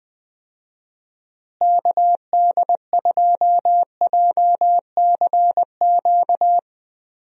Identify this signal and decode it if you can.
Morse code, heard as a keyed tone, about 20 wpm, 700 Hz. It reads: KD2JCQ
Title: USA Call Signs - 6 Characters - Repeated 3x - 20wpm